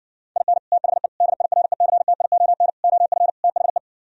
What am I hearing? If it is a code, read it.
Morse code, heard as a keyed tone, about 69 wpm, 700 Hz. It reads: IN THE DIRECTION OF THE